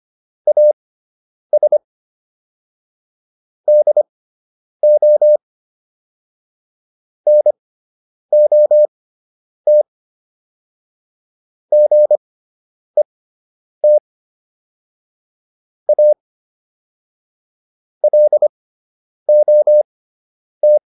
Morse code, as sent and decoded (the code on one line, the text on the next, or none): .- ... / -.. --- / -. --- - / --. - / .- / .-.. --- -
AS DO NOT GET A LOT